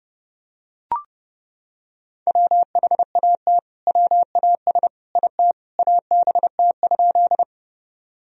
Morse code, heard as a keyed tone, about 30 wpm, 700 Hz. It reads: WHAT WAS IT ABT?
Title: Sentences from Top 100 Words and Abbreviations 30wpm